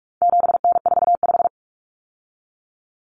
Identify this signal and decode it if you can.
Morse code, heard as a keyed tone, about 44 wpm, 700 Hz. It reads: 7N45
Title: Common Cut Numbers - 4-Digit 45wpm